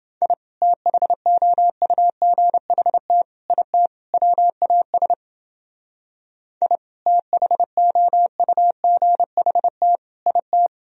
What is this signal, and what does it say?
Morse code, heard as a keyed tone, about 29 wpm, 700 Hz. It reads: I THOUGHT IT WAS I THOUGHT IT